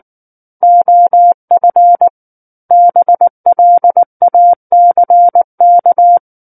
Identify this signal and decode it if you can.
Morse code, heard as a keyed tone, about 19 wpm, 700 Hz. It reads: OF BLACK
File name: Book One / 10-Chapter 10 - 19 WPM